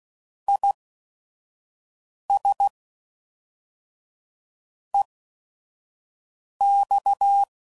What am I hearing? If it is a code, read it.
Morse code, heard as a keyed tone, about 16 wpm, spaced as for 5 wpm, 800 Hz. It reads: IS EX
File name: CW Code Groups @5WPM / Technical Subjects / AND Gate